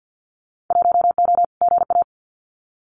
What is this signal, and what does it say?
Morse code, heard as a keyed tone, about 50 wpm, 700 Hz. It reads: 1O GA
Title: Winter Field Day Exchanges 50wpm